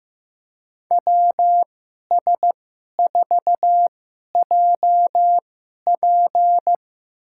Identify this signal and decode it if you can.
Morse code, heard as a keyed tone, about 15 wpm, spaced as for 11 wpm, 700 Hz. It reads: WS4JP
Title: USA Call Signs - 5 Characters - 2x Character Spacing - 15wpm